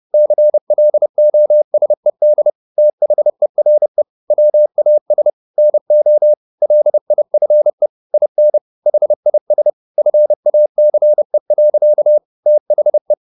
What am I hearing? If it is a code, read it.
Morse code, heard as a keyed tone, about 30 wpm, 600 Hz. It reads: CLOSED THERE WAS NO LIFE IN HIS FACE. THE
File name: the old man and the sea / the old man and the sea 1 30wpm